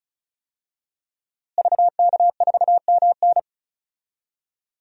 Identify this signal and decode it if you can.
Morse code, heard as a keyed tone, about 36 wpm, 700 Hz. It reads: VK4MN
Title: DXCC-100 Call Signs - 5 Characters - Repeated 2x - 35wpm